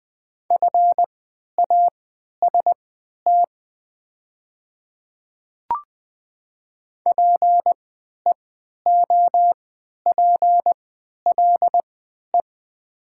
Morse code, heard as a keyed tone, about 20 wpm, spaced as for 11 wpm, 700 Hz. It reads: FAST PEOPLE
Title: 3x Character Spacing - Top 500 Words - 20wpm